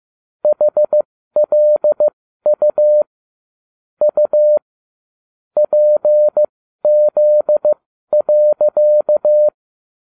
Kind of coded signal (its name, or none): Morse code